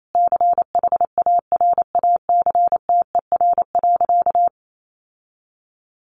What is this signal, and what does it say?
Morse code, heard as a keyed tone, about 28 wpm, 700 Hz. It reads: CHARACTER.
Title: Sentences from Top 1400 Words 28wpm